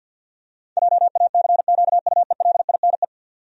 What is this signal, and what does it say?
Morse code, heard as a keyed tone, about 47 wpm, 700 Hz. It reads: JACQUELINE
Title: USA Common Names - 50wpm